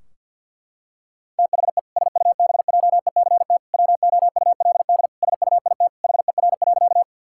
Morse code, heard as a keyed tone, about 50 wpm, 700 Hz. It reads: THE SUBJECT WOULD SUIT HER.
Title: Sentences from Top 1600 Words 50wpm